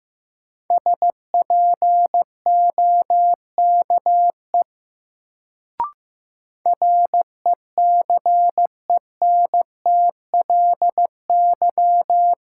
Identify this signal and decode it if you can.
Morse code, heard as a keyed tone, about 15 wpm, 700 Hz. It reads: SPOKE RECENTLY
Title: Sentences from Top 1200 Words - New Word Drill Alt - 15wpm